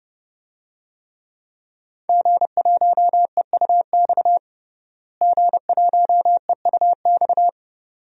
Morse code, heard as a keyed tone, about 30 wpm, 700 Hz. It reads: G1EUX G1EUX
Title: DXCC-100 Call Signs - 5 Characters - Repeated 2x - 30wpm